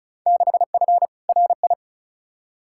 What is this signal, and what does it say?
Morse code, heard as a keyed tone, about 36 wpm, 700 Hz. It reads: 6F RI
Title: Field Day Exchanges Encore - 35wpm